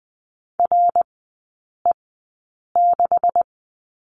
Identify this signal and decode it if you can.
Morse code, heard as a keyed tone, about 20 wpm, spaced as for 8 wpm, 700 Hz. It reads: RE6
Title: 172 Lesson 28e - Number 9 - Three Characters 20wpm